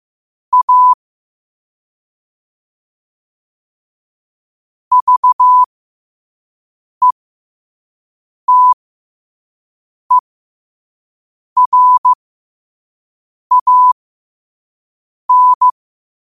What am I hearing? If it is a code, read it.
Morse code, heard as a keyed tone, about 15 wpm, spaced as for 5 wpm, 1000 Hz. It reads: A VETERAN
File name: qotdfiles / 5wpm